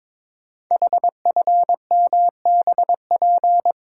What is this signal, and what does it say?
Morse code, heard as a keyed tone, about 22 wpm, 700 Hz. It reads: HFMBP
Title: Code Groups - 5 Letter:Numbers - 22wpm